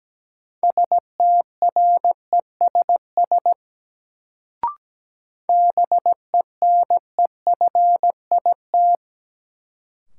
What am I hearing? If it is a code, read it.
Morse code, heard as a keyed tone, about 17 wpm, 700 Hz. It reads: STRESS BENEFIT
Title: Sentences from Top 1000 Words - Review Part 6 - New Word Drill - 17wpm